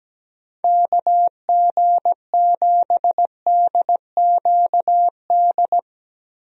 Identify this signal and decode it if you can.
Morse code, heard as a keyed tone, about 17 wpm, 700 Hz. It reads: KG7DQD
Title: USA Call Signs - 6 Characters - Repeated 2x - 17wpm